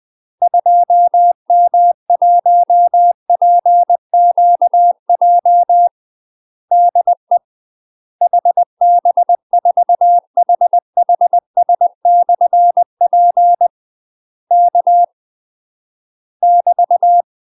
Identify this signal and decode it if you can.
Morse code, heard as a keyed tone, about 20 wpm, 700 Hz. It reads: 2M1PQJ DE HB4HHS/P K <BT>
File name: UK MASTER QSOs TRAINING / 20WPM / UK MASTER QSOs 0019